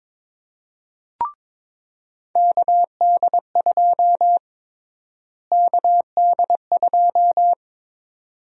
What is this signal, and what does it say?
Morse code, heard as a keyed tone, about 21 wpm, 700 Hz. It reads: KD2 KD2
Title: USA Call Sign Prefixes - Repeated 2x - 22wpm